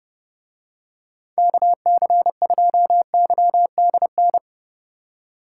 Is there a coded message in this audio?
Yes: Morse code from 1.4 to 4.4 s, heard as a keyed tone, about 30 wpm, 700 Hz. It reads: KC2YDN